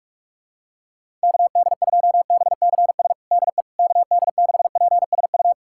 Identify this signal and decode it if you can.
Morse code, heard as a keyed tone, about 46 wpm, 700 Hz. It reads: KD2BCS DE KD6PSU
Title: USA Callsign DE Callsign 45wpm